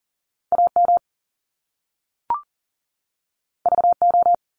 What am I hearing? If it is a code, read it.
Morse code, heard as a keyed tone, about 42 wpm, 700 Hz. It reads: AM VO
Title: Top Two Letter Combinations in Words 40wpm